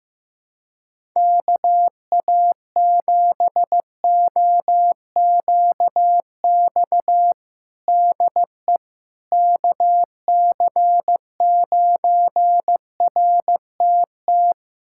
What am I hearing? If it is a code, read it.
Morse code, heard as a keyed tone, about 15 wpm, 700 Hz. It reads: KA7OQX DE KC9RTT
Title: USA Callsign DE Callsign - 6 Characters - 15wpm